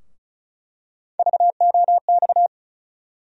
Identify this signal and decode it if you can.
Morse code, heard as a keyed tone, about 34 wpm, 700 Hz. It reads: VOX
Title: USA Call Sign Postfixes - 3 Letters 35wpm